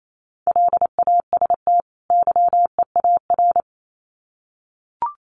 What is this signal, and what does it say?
Morse code, heard as a keyed tone, about 28 wpm, 700 Hz. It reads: LAST YEAR
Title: Sets of 2 Words 28wpm